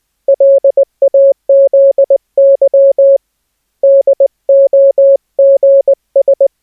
Morse code, heard as a keyed tone, about 20 wpm, 550 Hz. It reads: LAZY DOGS
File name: cw / cwfox / Op D 550 Hz